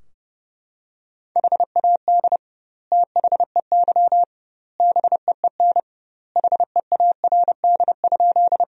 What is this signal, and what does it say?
Morse code, heard as a keyed tone, about 30 wpm, 700 Hz. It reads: HAD THEY BEEN HEARD?
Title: 2x Word-Spacing - Sentences from Top 400 Words - 30wpm